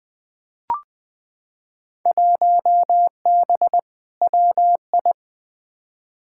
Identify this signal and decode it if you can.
Morse code, heard as a keyed tone, about 20 wpm, 700 Hz. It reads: 1B WI
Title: Field Day Exchanges - 20wpm